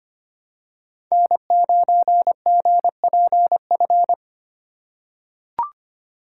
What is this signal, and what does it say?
Morse code, heard as a keyed tone, about 25 wpm, 700 Hz. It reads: N9GPF
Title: USA Call Signs - 5 Characters - 3x Character Spacing - 25wpm